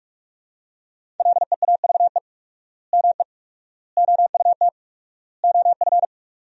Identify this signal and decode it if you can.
Morse code, heard as a keyed tone, about 41 wpm, 700 Hz. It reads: LEAVE ME OUT OF
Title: 4x Word-Spacing - Sentences from Top 400 Words - 45wpm